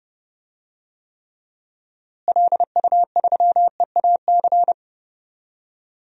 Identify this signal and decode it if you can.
Morse code, heard as a keyed tone, about 30 wpm, 700 Hz. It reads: LU3EAC
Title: DXCC-100 Call Signs - 6 Characters - Repeated 2x - 30wpm